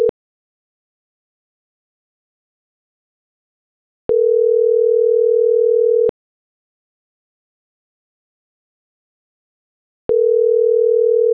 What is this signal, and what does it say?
Ringback tone (North America), 440 + 480 Hz.